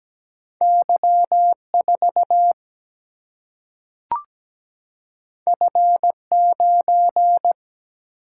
Morse code, heard as a keyed tone, about 17 wpm, 700 Hz. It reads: Y 4 F 9
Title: Single-Letter Single-Number 17wpm